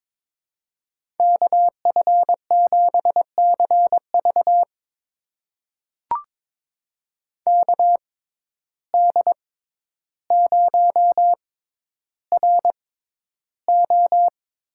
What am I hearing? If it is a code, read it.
Morse code, heard as a keyed tone, about 22 wpm, spaced as for 11 wpm, 700 Hz. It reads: KF7CV KD0RO
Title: USA Call Signs - 5 Characters - 6x Character Spacing - 22wpm